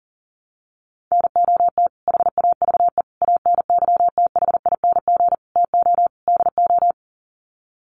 Morse code, heard as a keyed tone, about 40 wpm, 700 Hz. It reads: NOT HAVE ANYTHING TO DO
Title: Sets of 5 Words Encore 40wpm